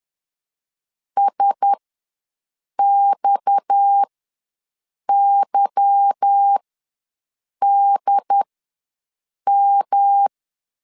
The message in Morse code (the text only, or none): SXYDM